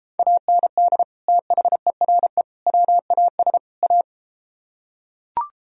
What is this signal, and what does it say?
Morse code, heard as a keyed tone, about 33 wpm, 700 Hz. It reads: AND THERE WAS A